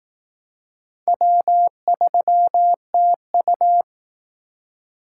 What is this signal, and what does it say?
Morse code, heard as a keyed tone, about 18 wpm, 700 Hz. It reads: W3TU
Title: Speed Racing - USA 1x2 Call Signs 15wpm